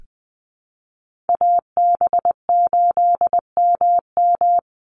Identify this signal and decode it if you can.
Morse code, heard as a keyed tone, about 20 wpm, 700 Hz. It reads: AB8MM
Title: USA Call Signs 20wpm